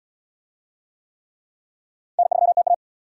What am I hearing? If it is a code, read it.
Morse code, heard as a keyed tone, about 78 wpm, 700 Hz. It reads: N3TA